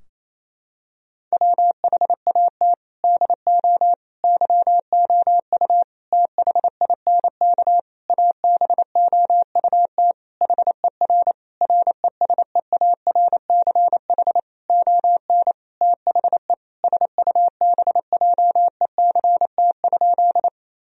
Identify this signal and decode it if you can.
Morse code, heard as a keyed tone, about 28 wpm, 700 Hz. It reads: WHAT DO YOU THINK ABOUT HER RESEARCH ON THE SUBJECT?
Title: Sentences from Top 1000 Words - Review Part 4 - 28wpm